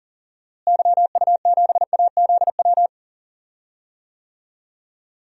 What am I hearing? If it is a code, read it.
Morse code, heard as a keyed tone, about 40 wpm, 700 Hz. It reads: YU7AZW
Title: DXCC-100 Call Signs - 6 Characters - Repeated 3x - 40wpm